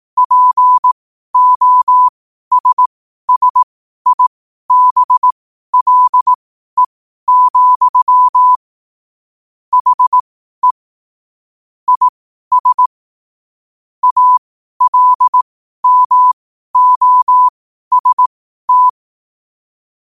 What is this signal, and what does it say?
Morse code, heard as a keyed tone, about 18 wpm, spaced as for 12 wpm, 1000 Hz. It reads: POSSIBLE, HE IS ALMOST
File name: qotdfiles / 12wpm